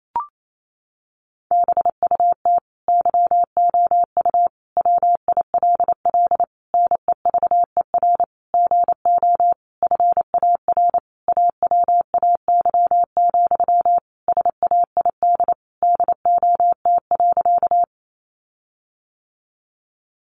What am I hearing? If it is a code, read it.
Morse code, heard as a keyed tone, about 28 wpm, 700 Hz. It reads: BUT YOU WILL NEVER GO FAR AWAY, SAID DOT.
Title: Sentences from Top 1500 Words 28wpm